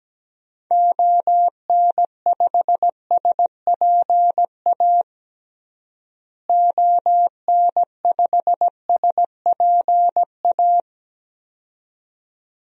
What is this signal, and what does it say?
Morse code, heard as a keyed tone, about 17 wpm, 700 Hz. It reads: ON5SPA ON5SPA
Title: DXCC-100 Call Signs - 6 Characters - Repeated 2x - 17wpm